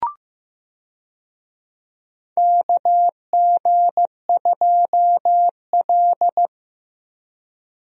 Morse code, heard as a keyed tone, about 16 wpm, 700 Hz. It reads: KG2L
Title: USA Call Signs - 4 Characters - Repeated 2x - 15wpm